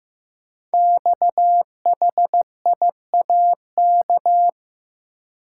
Morse code, heard as a keyed tone, about 15 wpm, 700 Hz. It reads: XHIAK
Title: Code Groups - 5 Letter:Numbers - 15wpm